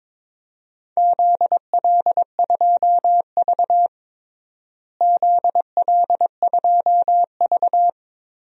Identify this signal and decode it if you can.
Morse code, heard as a keyed tone, about 22 wpm, 700 Hz. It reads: ZL2V ZL2V
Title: DXCC-100 Call Signs - 4 Characters - Repeated 2x - 22wpm